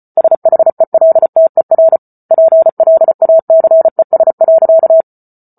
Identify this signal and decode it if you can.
Morse code, heard as a keyed tone, about 34 wpm, 650 Hz. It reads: SHELTER PLACES.